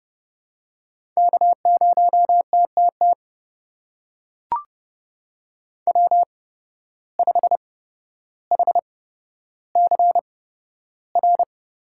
Morse code, heard as a keyed tone, about 32 wpm, spaced as for 13 wpm, 700 Hz. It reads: K0TTT W5HCR
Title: USA Call Signs - 5 Characters - 8x Character Spacing - 30wpm